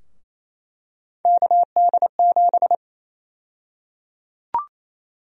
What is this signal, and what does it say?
Morse code, heard as a keyed tone, about 28 wpm, 700 Hz. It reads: KD7
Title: USA Call Sign Prefixes - Repeated 2x - 28wpm